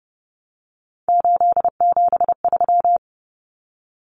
Morse code, heard as a keyed tone, about 30 wpm, 700 Hz. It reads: 873
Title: Common Cut Numbers - 3-Digit 30wpm